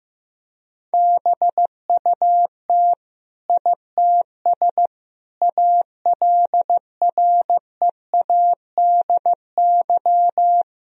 Morse code, heard as a keyed tone, about 15 wpm, 700 Hz. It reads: BUT ITS ALREADY
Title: Sentences from Top 1000 Words - Review Part 2 - 15wpm